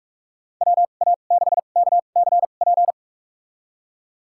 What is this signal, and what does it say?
Morse code, heard as a keyed tone, about 41 wpm, spaced as for 31 wpm, 700 Hz. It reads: WA6KCP
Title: USA Call Signs - 6 Characters - 2x Character Spacing - 45wpm